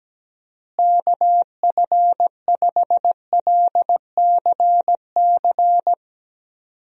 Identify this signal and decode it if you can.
Morse code, heard as a keyed tone, about 17 wpm, 700 Hz. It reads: KF5LCC
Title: USA Call Signs - 6 Characters - Repeated 2x - 17wpm